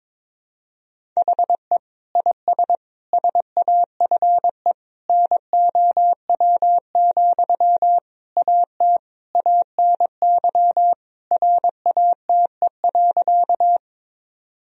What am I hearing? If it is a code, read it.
Morse code, heard as a keyed tone, about 22 wpm, 700 Hz. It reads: HE IS SAFE NOW, AT ANY RATE.
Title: Sentences from Top 1000 Words - Review Part 4 - 22wpm